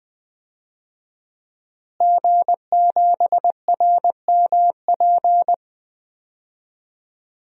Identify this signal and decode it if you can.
Morse code, heard as a keyed tone, about 20 wpm, 700 Hz. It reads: G7RMP
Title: DXCC-100 Call Signs - 5 Characters - Repeated 3x - 20wpm